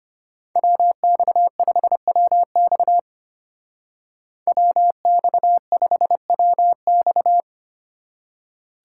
Morse code, heard as a keyed tone, about 33 wpm, 700 Hz. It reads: WX5WX WX5WX